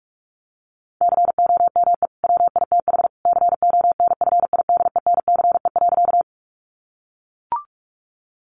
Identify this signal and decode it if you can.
Morse code, heard as a keyed tone, about 44 wpm, 700 Hz. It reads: COME WITH CONFIDENCE.
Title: Sentences from Top 1500 Words 45wpm